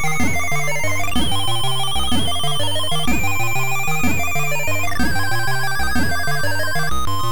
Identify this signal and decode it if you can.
There is no code.